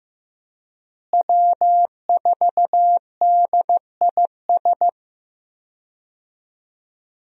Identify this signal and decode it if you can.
Morse code, heard as a keyed tone, about 16 wpm, 700 Hz. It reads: W4DIS
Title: USA Call Signs - 5 Characters - Repeated 3x - 15wpm